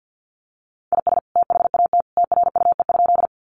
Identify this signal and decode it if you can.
Morse code, heard as a keyed tone, about 50 wpm, 700 Hz. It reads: IS THAT TRUE?
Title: Sentences from Top 400 Words 50wpm